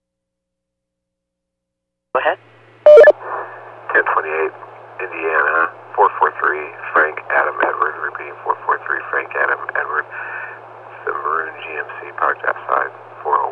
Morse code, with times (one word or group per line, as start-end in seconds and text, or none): none